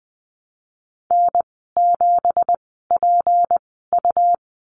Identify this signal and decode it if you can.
Morse code, heard as a keyed tone, about 20 wpm, spaced as for 15 wpm, 700 Hz. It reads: N7PU
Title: 173 Lesson 28g - Number 9 - Call Signs 20wpm